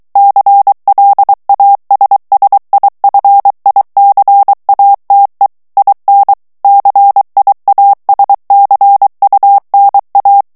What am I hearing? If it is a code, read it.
Morse code, heard as a keyed tone, about 23 wpm, 800 Hz. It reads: CLASSIFICATE IN CIASCUNA
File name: ITALIEN texte clair 2 vitesse 22 mots minute